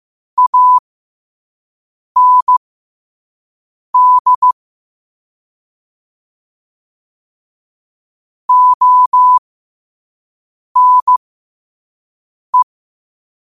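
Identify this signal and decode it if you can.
Morse code, heard as a keyed tone, about 15 wpm, spaced as for 5 wpm, 1000 Hz. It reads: AND ONE
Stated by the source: Morse code sent at 5 wpm.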